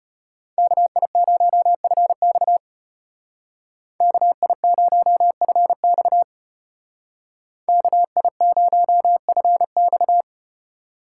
Morse code, heard as a keyed tone, about 35 wpm, 700 Hz. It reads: KI0FX KI0FX KI0FX